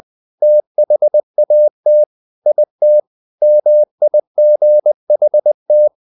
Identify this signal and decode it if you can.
Morse code, heard as a keyed tone, about 20 wpm, 600 Hz. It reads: THAT IT MIGHT
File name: the old man and the sea part 2 20wpm